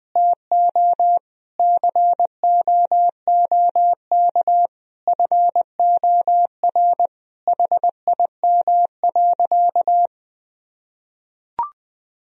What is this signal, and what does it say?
Morse code, heard as a keyed tone, about 20 wpm, 700 Hz. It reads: TO COOK FOR HIM.
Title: Sentences from Top 1000 Words - Review Part 4 - 20wpm